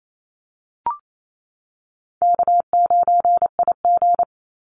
Morse code, heard as a keyed tone, about 28 wpm, 700 Hz. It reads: K9IG
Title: Easy Call Signs - 28wpm